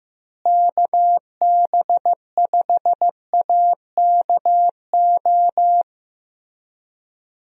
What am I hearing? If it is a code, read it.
Morse code, heard as a keyed tone, about 15 wpm, 700 Hz. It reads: KB5AKO
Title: USA Call Signs - 6 Characters - Repeated 2x - 15wpm